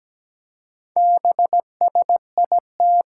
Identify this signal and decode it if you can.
Morse code, heard as a keyed tone, about 17 wpm, 700 Hz. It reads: BSIT
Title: Four Challenging Letter-Number Combinations 17wpm